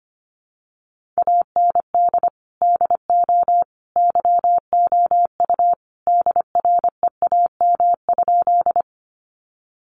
Morse code, heard as a keyed tone, about 25 wpm, 700 Hz. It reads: AND DO YOU DREAM?
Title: Sentences from Top 1500 Words 25wpm